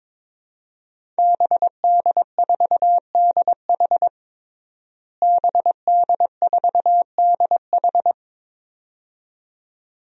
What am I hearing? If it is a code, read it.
Morse code, heard as a keyed tone, about 22 wpm, 700 Hz. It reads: BD4DH BD4DH
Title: DXCC-100 Call Signs - 5 Characters - Repeated 2x - 22wpm